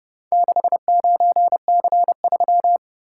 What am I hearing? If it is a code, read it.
Morse code, heard as a keyed tone, about 29 wpm, 700 Hz. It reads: 69C3